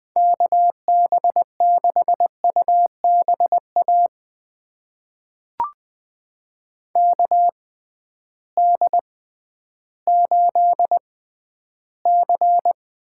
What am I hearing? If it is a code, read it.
Morse code, heard as a keyed tone, about 20 wpm, spaced as for 12 wpm, 700 Hz. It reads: KB6UBA KD8C
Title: USA Call Signs - 6 Characters - 6x Character Spacing - 20wpm